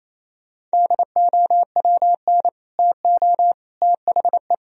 Morse code, heard as a keyed tone, about 37 wpm, 700 Hz. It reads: DOWN TO THE